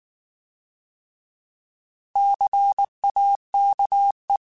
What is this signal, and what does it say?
Morse code, heard as a keyed tone, about 19 wpm, 800 Hz. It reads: CAKE
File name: Alphabetical / Alphabetical mp3s / Cs @18